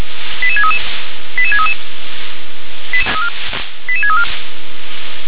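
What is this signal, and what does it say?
Five-tone selective call (ZVEI), heard as quick runs of five tones, tones of about 70 ms. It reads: 88633 88633 88634
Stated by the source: Five-tone ZVEI selective call 88500, received by SDR.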